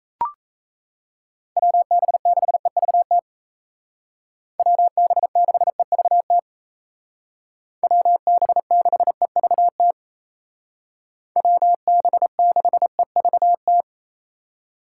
Morse code, heard as a keyed tone, about 34 wpm, 700 Hz. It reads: WB6EVT WB6EVT WB6EVT WB6EVT
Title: Speed Racing - USA 2x3 Call Signs 28wpm